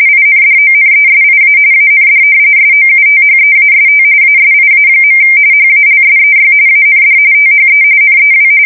RTTY, about 45.45 baud. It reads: INDONESIA AND SOUTHEAST ASIA. HALSEY AND THE NEW
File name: fsk170 / battleshipnj